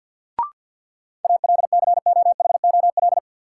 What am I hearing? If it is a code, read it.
Morse code, heard as a keyed tone, about 53 wpm, 700 Hz. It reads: ALCOHOL
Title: Sentences from Top 1400 Words - New Word Drill - 50wpm